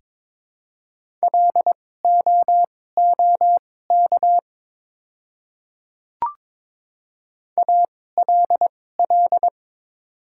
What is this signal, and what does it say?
Morse code, heard as a keyed tone, about 22 wpm, spaced as for 16 wpm, 700 Hz. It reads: LOOK ALL
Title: 2x Character Spacing - Top 200 Words - 22wpm